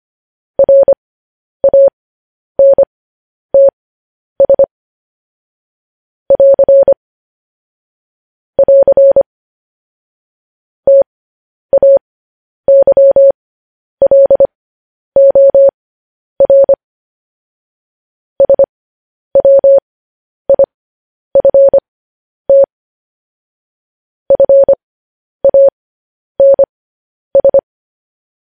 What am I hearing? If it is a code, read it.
Morse code, heard as a keyed tone, about 25 wpm, spaced as for 10 wpm, 550 Hz. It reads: RANTS <AR> <AR> TAYLOR SWIFT FANS